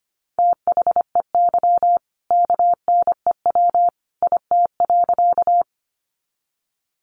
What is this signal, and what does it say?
Morse code, heard as a keyed tone, about 25 wpm, 700 Hz. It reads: THEY KNEW IT.
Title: Sentences from Top 400 Words 25wpm